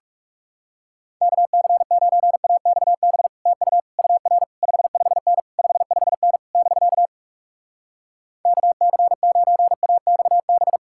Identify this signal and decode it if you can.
Morse code, heard as a keyed tone, about 40 wpm, 700 Hz. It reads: KC9AXB TU UR 55N 55N <BK> KC9AXB